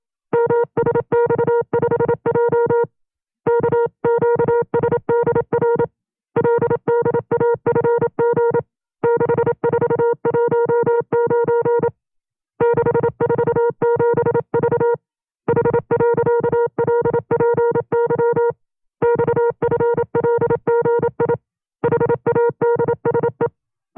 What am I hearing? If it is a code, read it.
Morse code, heard as a keyed tone, about 27 wpm, 470 Hz. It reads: MSX5J KQSDR LDAFG 6419 647V H.LPY XFLGI HADSE
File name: FreeFiles / CW-practice / 25WPM-side1